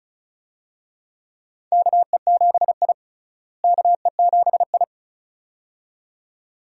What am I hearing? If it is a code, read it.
Morse code, heard as a keyed tone, about 36 wpm, 700 Hz. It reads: KE7I KE7I
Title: USA Call Signs - 4 Characters - Repeated 2x - 35wpm